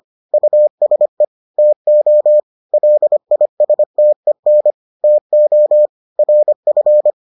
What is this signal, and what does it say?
Morse code, heard as a keyed tone, about 25 wpm, 600 Hz. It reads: USE TO LISTEN TO RF